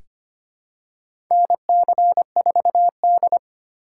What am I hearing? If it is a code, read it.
Morse code, heard as a keyed tone, about 25 wpm, 700 Hz. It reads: NC4D
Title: USA Call Signs - 4 Characters - 3x Character Spacing - 25wpm